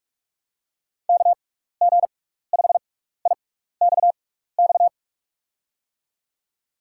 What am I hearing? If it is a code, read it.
Morse code, heard as a keyed tone, about 44 wpm, spaced as for 15 wpm, 700 Hz. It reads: KG5IXX
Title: USA Call Signs - 6x Character Spacing - 45wpm